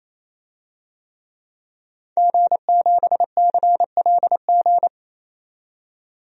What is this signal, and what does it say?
Morse code, heard as a keyed tone, about 28 wpm, 700 Hz. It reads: G7CLG
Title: DXCC-100 Call Signs - 5 Characters - 28wpm